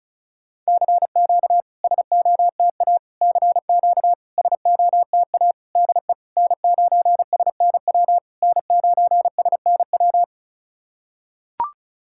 Morse code, heard as a keyed tone, about 36 wpm, 700 Hz. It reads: CQ SOTA CQ SOTA DE N9SNW N9SNW
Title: SOTA QSO Practice 35wpm